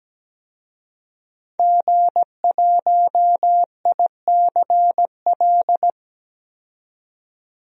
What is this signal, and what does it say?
Morse code, heard as a keyed tone, about 17 wpm, 700 Hz. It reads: G1ICL
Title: DXCC-100 Call Signs - 5 Characters - 17wpm